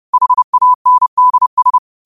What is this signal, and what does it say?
Morse code, heard as a keyed tone, about 30 wpm, 1000 Hz. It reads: HANDS
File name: qotdfiles / 30wpm